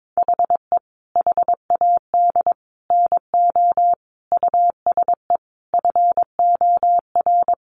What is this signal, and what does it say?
Morse code, heard as a keyed tone, about 22 wpm, 700 Hz. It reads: HE HAD NO USE FOR
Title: Sentences from Top 1200 Words 22wpm